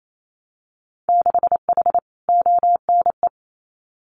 Morse code, heard as a keyed tone, about 28 wpm, 700 Hz. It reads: 6H ONE
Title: Winter Field Day Exchanges Encore 28wpm